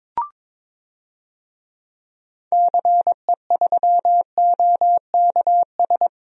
Morse code, heard as a keyed tone, about 22 wpm, 700 Hz. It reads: CE3OKS
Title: DXCC-100 Call Signs - 6 Characters - Repeated 2x - 22wpm